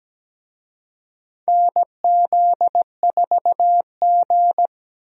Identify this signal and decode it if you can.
Morse code, heard as a keyed tone, about 17 wpm, 700 Hz. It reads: NZ4G
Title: USA Call Signs - 4 Characters - 17wpm